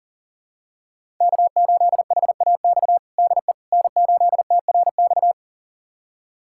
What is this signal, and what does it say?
Morse code, heard as a keyed tone, about 40 wpm, 700 Hz. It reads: K8HAX DE N8TRX
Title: USA Callsign DE Callsign - 5 Characters - 40wpm